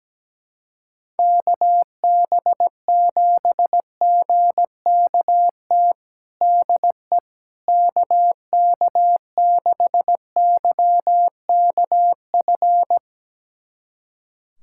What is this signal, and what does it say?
Morse code, heard as a keyed tone, about 17 wpm, 700 Hz. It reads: KB7GKT DE KK6YKF
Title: USA Callsign DE Callsign 17wpm